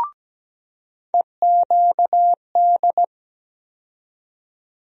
Morse code, heard as a keyed tone, about 17 wpm, 700 Hz. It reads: EQD
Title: USA Call Sign Postfixes - 3 Letters 17wpm